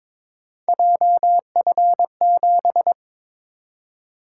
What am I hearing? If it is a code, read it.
Morse code, heard as a keyed tone, about 22 wpm, 700 Hz. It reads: JF7
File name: practice / 22wpm / Three Letter-Number Combinations - 22wpm